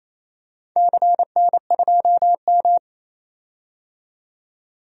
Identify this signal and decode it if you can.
Morse code, heard as a keyed tone, about 28 wpm, 700 Hz. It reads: CN2M
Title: DXCC-100 Call Signs - 4 Characters - 28wpm